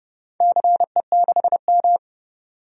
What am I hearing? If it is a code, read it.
Morse code, heard as a keyed tone, about 30 wpm, 700 Hz. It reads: CE6M
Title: DXCC-100 Call Signs - 4 Characters - 30wpm